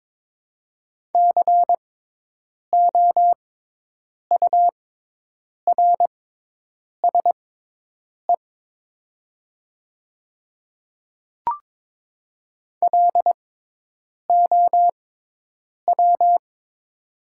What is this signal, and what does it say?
Morse code, heard as a keyed tone, about 22 wpm, spaced as for 8 wpm, 700 Hz. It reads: COURSE LOW
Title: 6x Character Spacing - Top 500 Words - 22wpm